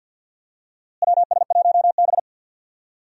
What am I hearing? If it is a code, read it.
Morse code, heard as a keyed tone, about 50 wpm, 700 Hz. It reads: WS1B
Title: USA Call Signs - 4 Characters - Repeated 2x - 50wpm